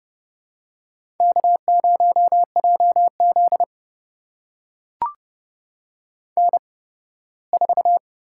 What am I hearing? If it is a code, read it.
Morse code, heard as a keyed tone, about 30 wpm, spaced as for 18 wpm, 700 Hz. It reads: K0JZ N4
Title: USA Call Signs - 4 Characters - 8x Character Spacing - 30wpm